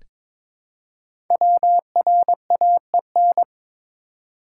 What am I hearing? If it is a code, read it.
Morse code, heard as a keyed tone, about 22 wpm, 700 Hz. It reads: WRAEN
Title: Code Groups - 5 Letters - 22wpm